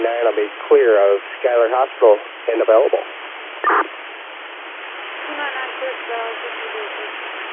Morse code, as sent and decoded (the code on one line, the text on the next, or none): none